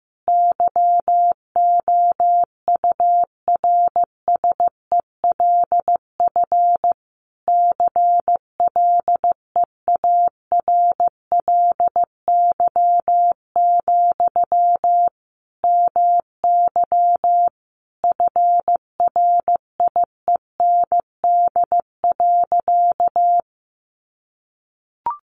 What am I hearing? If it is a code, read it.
Morse code, heard as a keyed tone, about 15 wpm, 700 Hz. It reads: YOURSELF CLEARLY, MY FRIEND.
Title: Sentences from Top 1300 Words 15wpm